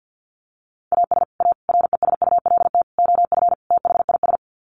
Morse code, heard as a keyed tone, about 50 wpm, 700 Hz. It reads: AS A RESULT OF THIS
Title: Sets of 5 Words 50wpm